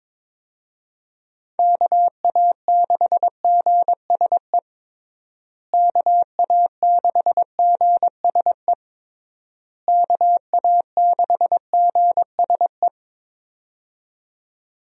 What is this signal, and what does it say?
Morse code, heard as a keyed tone, about 22 wpm, 700 Hz. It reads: KA6GSE KA6GSE KA6GSE
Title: USA Call Signs - 6 Characters - Repeated 3x - 22wpm